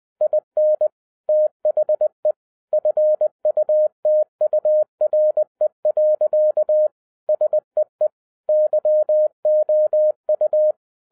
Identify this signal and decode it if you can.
Morse code, heard as a keyed tone, about 20 wpm, 610 Hz. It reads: IN THE FUTURE. SEE YOU
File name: 2012-01-mp3 / n05-CAME BACK TO HAM RADIO AFTER 30 YEARS OF INTERVAL-20wpm